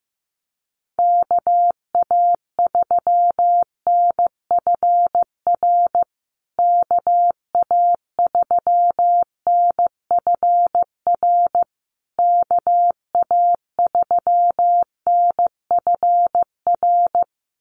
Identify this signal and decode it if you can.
Morse code, heard as a keyed tone, about 15 wpm, 700 Hz. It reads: KA3NFR KA3NFR KA3NFR
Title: USA Call Signs Repeated 3x 15wpm